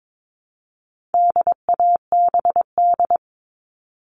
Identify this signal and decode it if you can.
Morse code, heard as a keyed tone, about 22 wpm, 700 Hz. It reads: DABD
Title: All Cut Numbers - 4-Digit 22wpm